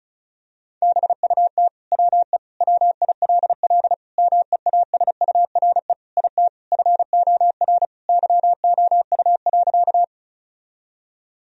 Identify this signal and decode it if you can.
Morse code, heard as a keyed tone, about 34 wpm, 700 Hz. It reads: BUT WE WILL MEASURE IT FOR YOU.
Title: Sentences from Top 1000 Words - Review Part 2 - 35wpm